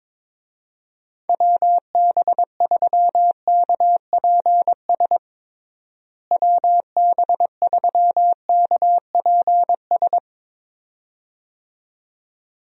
Morse code, heard as a keyed tone, about 22 wpm, 700 Hz. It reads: WB3KPS WB3KPS
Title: USA Call Signs - 6 Characters - Repeated 2x - 22wpm